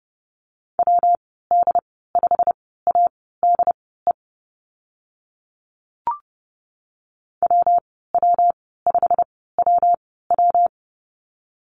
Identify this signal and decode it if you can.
Morse code, heard as a keyed tone, about 30 wpm, spaced as for 17 wpm, 700 Hz. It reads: WD5ADE WW5WW
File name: Lesson 19g - Letter w - Call Signs 30wpm